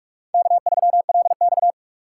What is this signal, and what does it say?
Morse code, heard as a keyed tone, about 44 wpm, 700 Hz. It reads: K3LX
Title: USA Call Signs - 8x Character Spacing - 45wpm